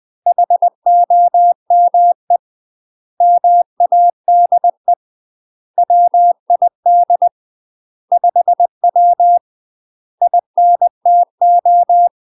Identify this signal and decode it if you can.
Morse code, heard as a keyed tone, about 20 wpm, 700 Hz. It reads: HOME MADE WID 5W INTO
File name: UK MASTER QSOs TRAINING / 20WPM / UK MASTER QSOs 0040